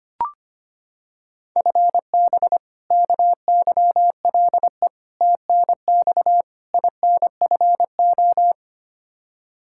Morse code, heard as a keyed tone, about 25 wpm, 700 Hz. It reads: FB KYLE TNX INFO
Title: Practice QSO - Exchange 3 - More Info - 25wpm